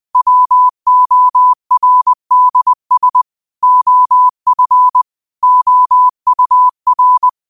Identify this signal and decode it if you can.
Morse code, heard as a keyed tone, about 20 wpm, 1000 Hz. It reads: WORDS OF OUR